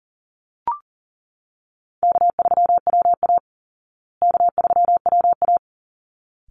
Morse code, heard as a keyed tone, about 40 wpm, 700 Hz. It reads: K3WA K3WA
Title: Call Signs Repeated 3x - 40wpm